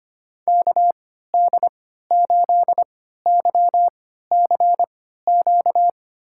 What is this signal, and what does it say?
Morse code, heard as a keyed tone, about 25 wpm, spaced as for 14 wpm, 700 Hz. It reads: KD8YCQ
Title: USA Call Signs - 3x Character Spacing - 25wpm